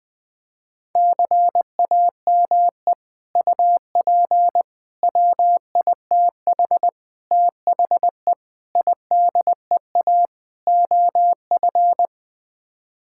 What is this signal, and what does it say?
Morse code, heard as a keyed tone, about 20 wpm, 700 Hz. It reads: CAME UP WITH THE IDEA OF
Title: Sets of 6 Words - 20wpm